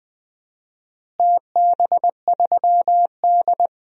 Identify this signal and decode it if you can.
Morse code, heard as a keyed tone, about 20 wpm, 700 Hz. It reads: TB3D
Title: DXCC-100 Call Signs - 4 Characters - Repeated 2x - 20wpm